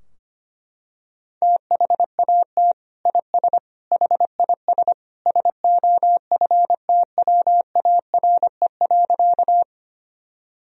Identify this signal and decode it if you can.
Morse code, heard as a keyed tone, about 25 wpm, 700 Hz. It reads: THAT IS HIS SOFTWARE.
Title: Sentences from Top 1000 Words - Review Part 3 - 25wpm